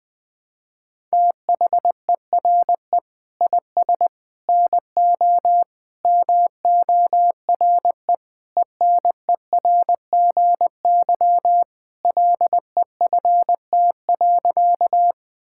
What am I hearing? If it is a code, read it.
Morse code, heard as a keyed tone, about 21 wpm, 700 Hz. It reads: THERE IS NO MORE ENERGY LEFT.
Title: Sentences from Top 1000 Words - Review Part 6 - 20wpm